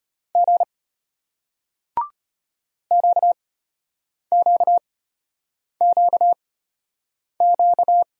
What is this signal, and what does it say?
Morse code, heard as a keyed tone, about 34 wpm, spaced as for 8 wpm, 700 Hz. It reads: G QQQQ